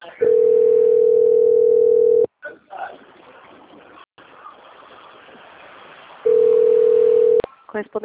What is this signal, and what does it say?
Ringback tone (North America), 440 + 480 Hz, with speech.
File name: out- - - - - .19590